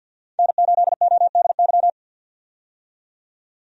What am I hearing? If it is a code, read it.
Morse code, heard as a keyed tone, about 52 wpm, 700 Hz. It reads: N8ODY
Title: USA Call Signs - 5 Characters - 50wpm